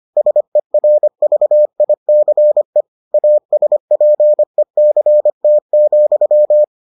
Morse code, heard as a keyed tone, about 25 wpm, 600 Hz. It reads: SERVICE ASPECT,